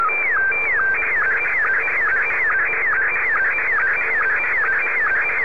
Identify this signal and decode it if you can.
SSTV (slow-scan television), Scottie 1.
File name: sstv / S1-VK4CAU